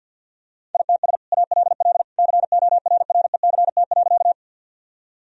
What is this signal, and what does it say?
Morse code, heard as a keyed tone, about 50 wpm, 700 Hz. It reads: ITS ALL CORRECT.
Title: Sentences from Top 1000 Words - Review Part 2 - 50wpm